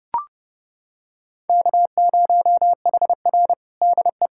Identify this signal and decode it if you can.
Morse code, heard as a keyed tone, about 30 wpm, 700 Hz. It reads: K0HR DE
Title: USA Callsign DE Callsign - 4 Characters - 30wpm